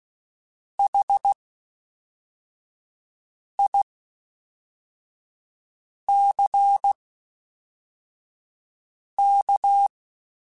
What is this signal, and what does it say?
Morse code, heard as a keyed tone, about 16 wpm, spaced as for 4 wpm, 800 Hz. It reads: HICK
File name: H Prefixes / H mp3s